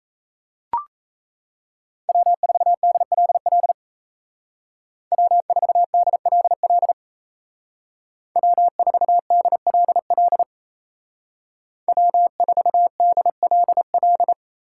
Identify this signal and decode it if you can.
Morse code, heard as a keyed tone, about 34 wpm, 700 Hz. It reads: W4DLL W4DLL W4DLL W4DLL
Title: Speed Racing - USA Call Signs 28wpm